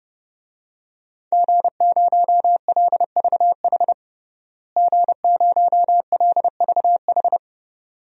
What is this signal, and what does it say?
Morse code, heard as a keyed tone, about 30 wpm, 700 Hz. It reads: G0LVH G0LVH
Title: DXCC-100 Call Signs - 5 Characters - Repeated 2x - 30wpm